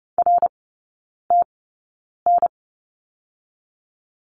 Morse code, heard as a keyed tone, about 30 wpm, spaced as for 9 wpm, 700 Hz. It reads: RTN
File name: Lesson 19e - Letter w - Three Characters 30wpm